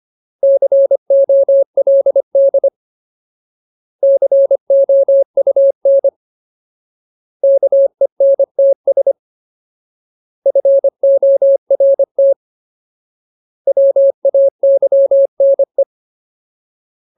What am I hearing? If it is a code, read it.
Morse code, heard as a keyed tone, about 25 wpm, 550 Hz. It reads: COLD COUN KENTS FORT WAYNE
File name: Ultimate Word List / 25WPM Ultimate